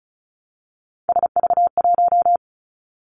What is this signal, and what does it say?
Morse code, heard as a keyed tone, about 36 wpm, 700 Hz. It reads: SV1
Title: Callsign Prefixes for Top 50 DXCC - 35wpm